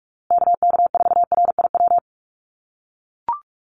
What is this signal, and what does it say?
Morse code, heard as a keyed tone, about 44 wpm, 700 Hz. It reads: KK4RIW
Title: USA Call Signs Repeated 3x 45wpm